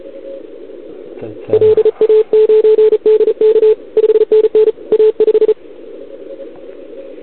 Morse code, heard as a keyed tone, about 33 wpm, 440 Hz. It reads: LA9DK 5NN A5
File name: CQWWCW2021 / OL3Z